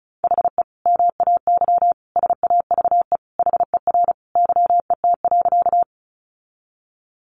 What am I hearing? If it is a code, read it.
Morse code, heard as a keyed tone, about 34 wpm, 700 Hz. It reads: HE MAY SAVE HER YET.
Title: Sentences from Top 1100 Words 35wpm